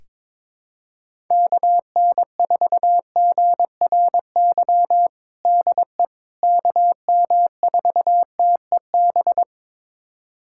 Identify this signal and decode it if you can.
Morse code, heard as a keyed tone, about 22 wpm, 700 Hz. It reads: KN4GRY DE KM4TEB
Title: USA Callsign DE Callsign 22wpm